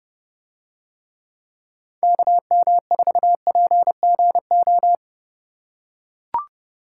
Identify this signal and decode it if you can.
Morse code, heard as a keyed tone, about 30 wpm, 700 Hz. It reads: KM4PGO